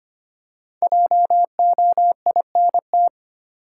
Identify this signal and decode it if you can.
Morse code, heard as a keyed tone, about 25 wpm, 700 Hz. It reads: JOINT
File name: 5 Letter Words - First 3 Letters Given - 25wpm